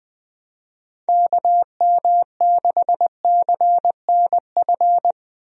Morse code, heard as a keyed tone, about 20 wpm, 700 Hz. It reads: KM6CNF
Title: USA Call Signs - 6 Characters - Repeated 2x - 20wpm